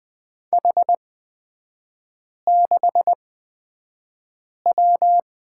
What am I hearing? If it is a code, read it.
Morse code, heard as a keyed tone, about 20 wpm, spaced as for 5 wpm, 700 Hz. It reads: H6W
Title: 255 Lesson 41e - Period - Three Characters 20wpm